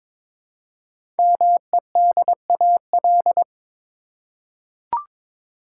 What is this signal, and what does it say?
Morse code, heard as a keyed tone, about 22 wpm, 700 Hz. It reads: MEDAL
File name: 5 Letter Words - First 3 Letters Given - 22wpm